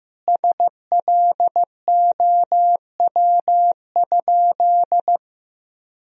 Morse code, heard as a keyed tone, about 15 wpm, 700 Hz. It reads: SLOW?
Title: Sentences from Top 1000 Words - Review Part 2 - 15wpm